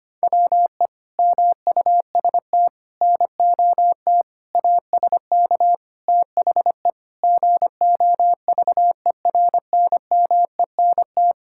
Morse code, heard as a keyed tone, about 25 wpm, 700 Hz. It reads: WE MUST NOT ASK THE GOVERNMENT
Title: Sentences from Top 1600 Words 25wpm